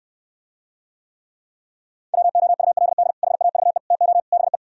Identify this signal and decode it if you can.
Morse code, heard as a keyed tone, about 78 wpm, 700 Hz. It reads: WOULD HAVE TO BE